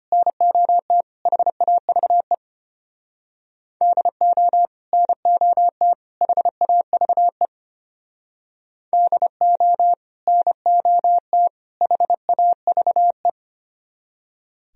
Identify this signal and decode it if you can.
Morse code, heard as a keyed tone, about 30 wpm, 700 Hz. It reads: NOT HAVE DO NOT HAVE DO NOT HAVE
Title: Speed-Racing - Sets of 3 Words 25wpm